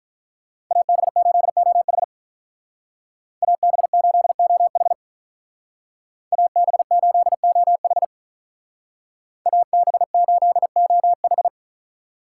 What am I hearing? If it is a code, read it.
Morse code, heard as a keyed tone, about 45 wpm, 700 Hz. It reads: AB8OH AB8OH AB8OH AB8OH